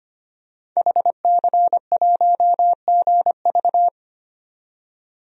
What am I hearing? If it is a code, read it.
Morse code, heard as a keyed tone, about 25 wpm, 700 Hz. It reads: HC1GV
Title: DXCC-100 Call Signs - 5 Characters - 25wpm